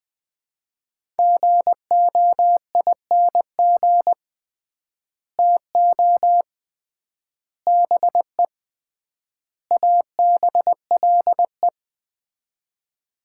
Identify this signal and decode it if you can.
Morse code, heard as a keyed tone, about 20 wpm, 700 Hz. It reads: GOING TO BE ABLE
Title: 3x Word-Spacing - Sets of 4 Words - 20wpm